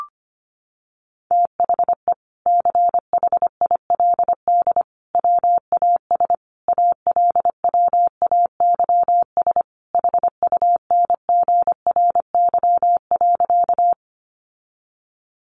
Morse code, heard as a keyed tone, about 25 wpm, 700 Hz. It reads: THE CHILD WAS ALWAYS HUNGRY.
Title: Sentences from Top 1400 Words 25wpm